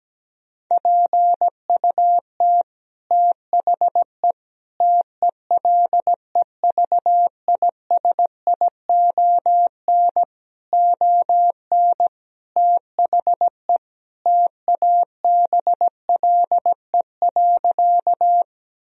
Morse code, heard as a keyed tone, about 17 wpm, 700 Hz. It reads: PUT THE TELEVISION ON THE TABLE.
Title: Sentences from Top 1000 Words - Review Part 4 - 17wpm